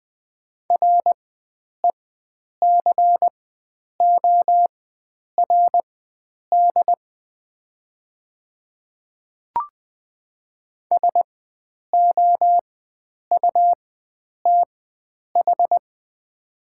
Morse code, heard as a keyed tone, about 20 wpm, spaced as for 9 wpm, 700 Hz. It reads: RECORD SOUTH
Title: 4x Character Spacing - Top 500 Words - 20wpm